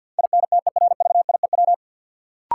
Morse code, heard as a keyed tone, about 53 wpm, 700 Hz. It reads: INTERVIEW
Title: Sentences from Top 1600 Words - New Word Drill - 50wpm